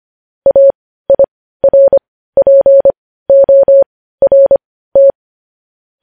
Morse code, heard as a keyed tone, about 25 wpm, spaced as for 15 wpm, 550 Hz. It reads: AIRPORT
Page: news headlines in morse code at 15 wpm